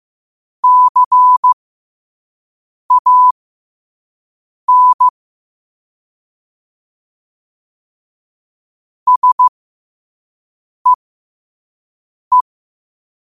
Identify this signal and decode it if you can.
Morse code, heard as a keyed tone, about 15 wpm, spaced as for 5 wpm, 1000 Hz. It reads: CAN SEE
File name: qotdfiles / 5wpm